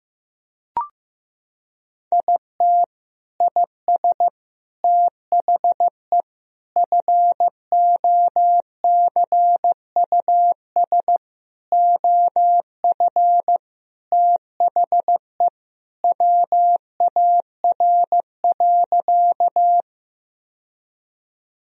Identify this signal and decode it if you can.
Morse code, heard as a keyed tone, about 15 wpm, 700 Hz. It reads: IT IS THE FOCUS OF THE WAR.
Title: Sentences from Top 1000 Words - Review Part 4 - 15wpm